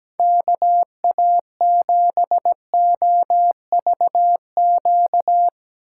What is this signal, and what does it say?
Morse code, heard as a keyed tone, about 17 wpm, 700 Hz. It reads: KA7OVQ
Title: USA Call Signs - 6 Characters - 4x Character Spacing - 17wpm